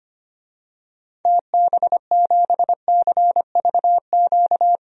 Morse code, heard as a keyed tone, about 25 wpm, 700 Hz. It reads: TB7CVQ
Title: DXCC-100 Call Signs - 6 Characters - 25wpm